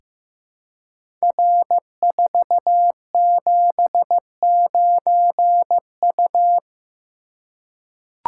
Morse code, heard as a keyed tone, about 15 wpm, 700 Hz. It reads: R479U
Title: Code Groups - 5 Letter:Numbers - 15wpm